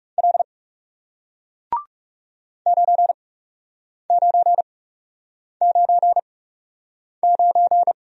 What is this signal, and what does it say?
Morse code, heard as a keyed tone, about 40 wpm, spaced as for 8 wpm, 700 Hz. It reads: L 9999